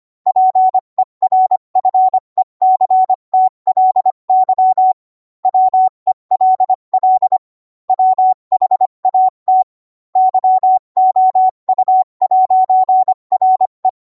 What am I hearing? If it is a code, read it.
Morse code, heard as a keyed tone, about 25 wpm, 750 Hz. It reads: PERFECTLY WELL WHAT YOU'RE